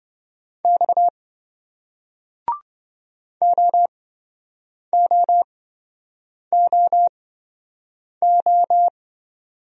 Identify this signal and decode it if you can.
Morse code, heard as a keyed tone, about 28 wpm, spaced as for 7 wpm, 700 Hz. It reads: X OOOO